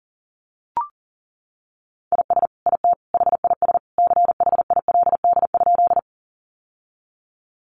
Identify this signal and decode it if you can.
Morse code, heard as a keyed tone, about 40 wpm, 700 Hz. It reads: IS IT HIS CHILD?
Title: Sentences from Top 800 Words 40wpm